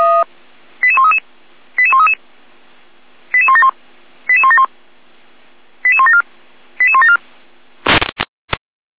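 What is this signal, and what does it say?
Five-tone selective call (ZVEI), heard as quick runs of five tones, tones of about 70 ms. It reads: 88133 88133 88171 88171 88174 88174
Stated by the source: ZVEI five-tone selective call 88217, received by SDR.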